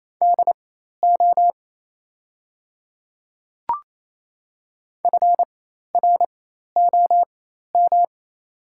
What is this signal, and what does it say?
Morse code, heard as a keyed tone, about 27 wpm, spaced as for 13 wpm, 700 Hz. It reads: DO FROM
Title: 4x Character Spacing - Top 300 Words - 28wpm